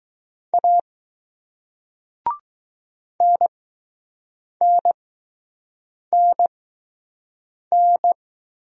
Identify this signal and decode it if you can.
Morse code, heard as a keyed tone, about 20 wpm, spaced as for 6 wpm, 700 Hz. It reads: A NNNN